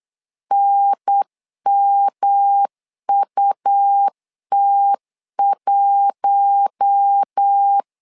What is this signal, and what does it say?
Morse code, heard as a keyed tone, about 8 wpm, 800 Hz. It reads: NMUT1
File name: LEZIONI / 4 Q1S3 8wpm 3037